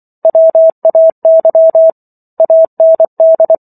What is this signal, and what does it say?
Morse code, heard as a keyed tone, about 24 wpm, 650 Hz. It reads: WAY AND